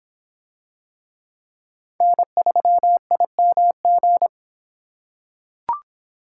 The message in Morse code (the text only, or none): N3IMG